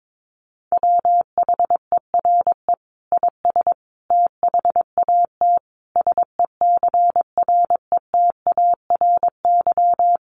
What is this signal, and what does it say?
Morse code, heard as a keyed tone, about 22 wpm, 700 Hz. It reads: WHERE IS THAT SECRETARY
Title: Sentences from Top 1300 Words 22wpm